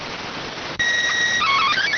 SSTV (slow-scan television), PD 290.